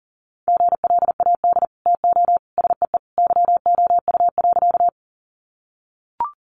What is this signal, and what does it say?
Morse code, heard as a keyed tone, about 40 wpm, 700 Hz. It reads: GLAD TO SEE YOU.
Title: Sentences from Top 800 Words 40wpm